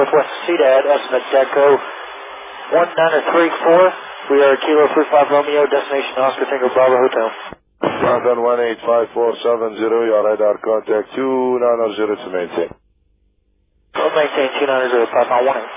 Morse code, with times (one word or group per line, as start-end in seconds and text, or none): none